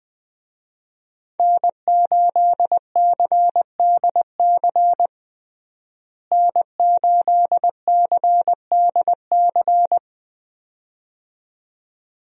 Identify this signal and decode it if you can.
Morse code, heard as a keyed tone, about 20 wpm, 700 Hz. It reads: N8CDC N8CDC
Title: USA Call Signs - 5 Characters - Repeated 2x - 20wpm